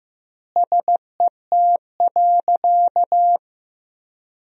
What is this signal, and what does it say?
Morse code, heard as a keyed tone, about 15 wpm, 700 Hz. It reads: SET.
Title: 2x Word-Spacing - Sentences from Top 400 Words - 15wpm